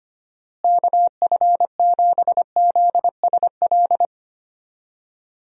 Morse code, heard as a keyed tone, about 25 wpm, 700 Hz. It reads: KF7ZSL
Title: USA Call Signs - 6 Characters - Repeated 2x - 25wpm